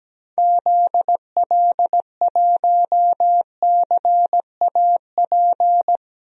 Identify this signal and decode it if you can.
Morse code, heard as a keyed tone, about 17 wpm, 700 Hz. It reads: ZL1CAP
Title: DXCC-100 Call Signs - 6 Characters - Repeated 2x - 17wpm